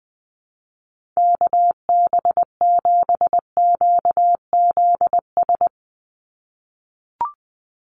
Morse code, heard as a keyed tone, about 20 wpm, 700 Hz. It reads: KB7QZS
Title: USA Call Signs 20wpm